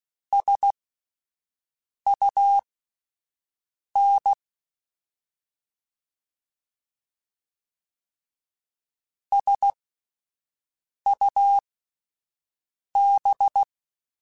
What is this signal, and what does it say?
Morse code, heard as a keyed tone, about 16 wpm, spaced as for 5 wpm, 800 Hz. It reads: SUN SUB